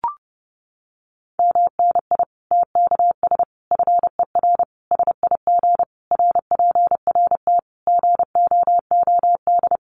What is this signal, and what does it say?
Morse code, heard as a keyed tone, about 28 wpm, 700 Hz. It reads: MNI TKS FER SIG RPRT GOOD
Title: Common QSO Phrases 30wpm